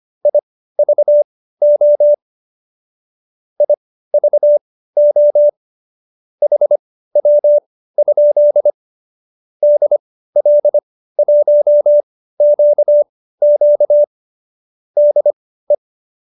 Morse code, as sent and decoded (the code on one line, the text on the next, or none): .. ...- --- / .. ...- --- / .... .-- ..--.. / -.. .-.. .---- --.- --.- / -.. .
IVO IVO HW? DL1QQ DE